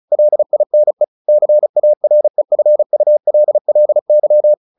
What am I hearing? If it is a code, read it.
Morse code, heard as a keyed tone, about 34 wpm, 600 Hz. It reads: LINE CAREFULLY
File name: the old man and the sea 2 35wpm